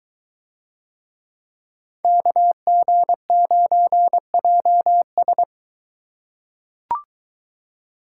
Morse code, heard as a keyed tone, about 23 wpm, 700 Hz. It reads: KG9JS